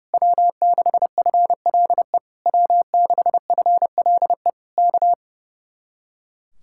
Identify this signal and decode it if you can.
Morse code, heard as a keyed tone, about 29 wpm, 700 Hz. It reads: W6FLE W6FLE K
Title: Speed-Racing Common QSO Phrases Encore - 30wpm